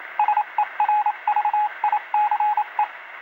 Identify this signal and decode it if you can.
Morse code, heard as a keyed tone, about 27 wpm, 840 Hz, with radio noise. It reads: SERVICE